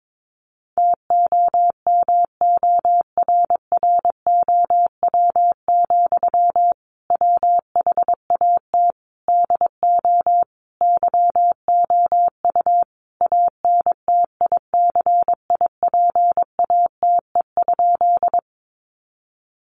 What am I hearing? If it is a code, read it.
Morse code, heard as a keyed tone, about 22 wpm, 700 Hz. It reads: TOMORROW, WHAT DO YOU ANTICIPATE?
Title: Sentences from Top 1500 Words 22wpm